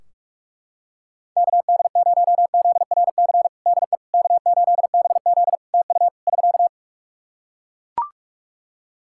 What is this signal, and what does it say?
Morse code, heard as a keyed tone, about 41 wpm, 700 Hz. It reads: KD0ZRC DE K8BZ TU <SK>
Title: Practice QSO - Exchange 4 - Ending - 45wpm